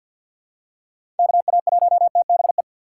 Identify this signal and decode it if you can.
Morse code, heard as a keyed tone, about 48 wpm, 700 Hz. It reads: KA1TBE
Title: USA Call Signs - 6x Character Spacing - 50wpm